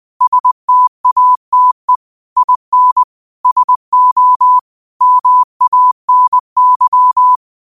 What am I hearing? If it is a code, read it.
Morse code, heard as a keyed tone, about 20 wpm, 1000 Hz. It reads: STATE IN SO MANY
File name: qotdfiles / 20wpm